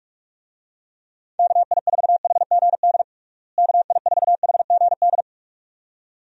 Morse code, heard as a keyed tone, about 46 wpm, 700 Hz. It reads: KI4HGD KI4HGD
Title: USA Call Signs - 6 Characters - Repeated 2x - 45wpm